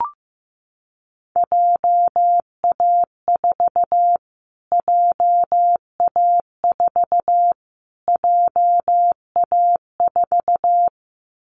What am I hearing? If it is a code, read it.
Morse code, heard as a keyed tone, about 15 wpm, 700 Hz. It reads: JA4 JA4 JA4
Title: Callsign Prefixes for Top 25 DXCC - Repeated 3x - 15wpm